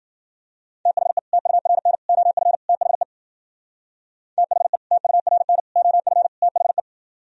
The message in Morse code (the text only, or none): THE TURN OF THE THE TURN OF THE